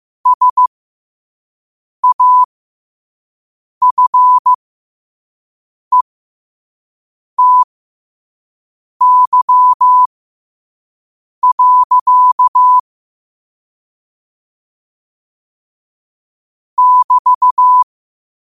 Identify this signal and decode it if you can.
Morse code, heard as a keyed tone, about 15 wpm, spaced as for 5 wpm, 1000 Hz. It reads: SAFETY. <BT>
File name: qotdfiles / 5wpm